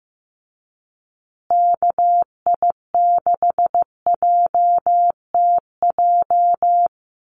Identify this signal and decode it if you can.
Morse code, heard as a keyed tone, about 15 wpm, 700 Hz. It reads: KI6JTJ
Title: USA Call Signs Repeated 2x 15wpm